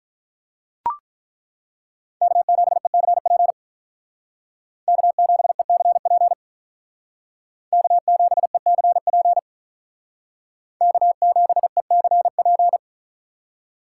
Morse code, heard as a keyed tone, about 42 wpm, 700 Hz. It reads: K7ECP K7ECP K7ECP K7ECP
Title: Speed Racing - USA Call Signs 35wpm